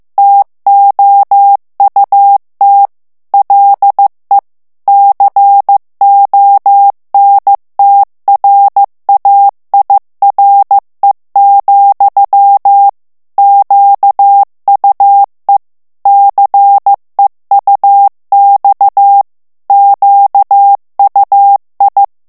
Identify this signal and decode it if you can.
Morse code, heard as a keyed tone, about 15 wpm, 800 Hz. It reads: TOUT LE CONTRAIRE, QUE CEUX QUI
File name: FRANCAIS TEXTE CLAIR 2 VITESSE 14 MOTS MINUTES